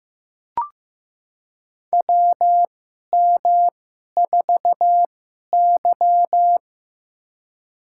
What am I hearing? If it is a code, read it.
Morse code, heard as a keyed tone, about 15 wpm, spaced as for 11 wpm, 700 Hz. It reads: WM4Y
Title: USA Call Signs - 4 Characters - 2x Character Spacing - 15wpm